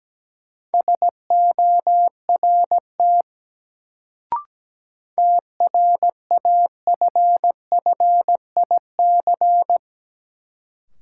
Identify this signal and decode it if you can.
Morse code, heard as a keyed tone, about 17 wpm, 700 Hz. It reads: SORT TRAFFIC
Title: Sentences from Top 1300 Words - New Word Drill Alt - 17wpm